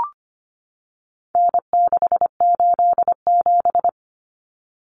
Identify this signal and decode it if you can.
Morse code, heard as a keyed tone, about 25 wpm, 700 Hz. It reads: N687
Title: Common Cut Numbers - 4-Digit 25wpm